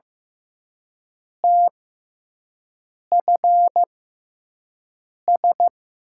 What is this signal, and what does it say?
Morse code, heard as a keyed tone, about 15 wpm, spaced as for 5 wpm, 700 Hz. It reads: TFS
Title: USA Call Signs - 5 Characters - 6x Character Spacing - 15wpm